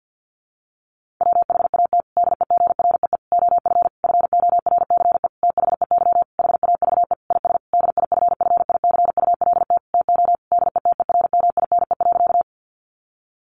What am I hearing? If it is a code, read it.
Morse code, heard as a keyed tone, about 50 wpm, 700 Hz. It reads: WHAT DEGREE OF FORCE THEY HAVE IS DIFFICULT TO DETERMINE.
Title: Sentences from Top 1500 Words 50wpm